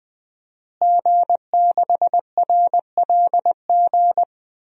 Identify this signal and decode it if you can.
Morse code, heard as a keyed tone, about 20 wpm, 700 Hz. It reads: G6RLG
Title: DXCC-100 Call Signs - 5 Characters - Repeated 3x - 20wpm